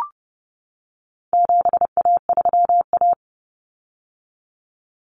Morse code, heard as a keyed tone, about 30 wpm, 700 Hz. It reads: 7A3A
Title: Common Cut Numbers - 4-Digit 30wpm